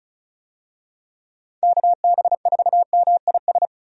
Morse code, heard as a keyed tone, about 35 wpm, 700 Hz. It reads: KB4MIS
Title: USA Call Signs - 6 Characters - 35wpm